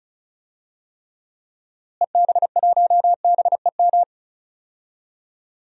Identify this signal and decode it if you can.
Morse code, heard as a keyed tone, about 34 wpm, 700 Hz. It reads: EB1BEM
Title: DXCC-100 Call Signs - 6 Characters - Repeated 3x - 35wpm